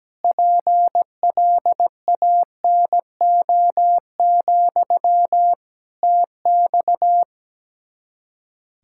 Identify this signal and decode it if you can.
Morse code, heard as a keyed tone, about 17 wpm, 700 Hz. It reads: PLANO, TX
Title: Practice QSO - Exchange 1 - Beginning - 17wpm